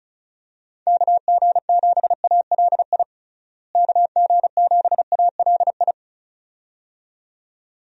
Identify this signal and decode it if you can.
Morse code, heard as a keyed tone, about 36 wpm, 700 Hz. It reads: KG7ALI KG7ALI
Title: USA Call Signs - 6 Characters - Repeated 2x - 35wpm